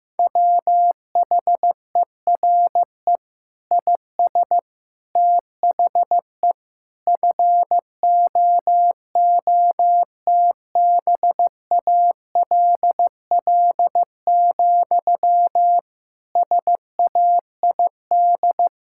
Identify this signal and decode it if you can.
Morse code, heard as a keyed tone, about 15 wpm, 700 Hz. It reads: WHERE IS THE FOOTBALL, SAID
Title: Sentences from Top 1600 Words 15wpm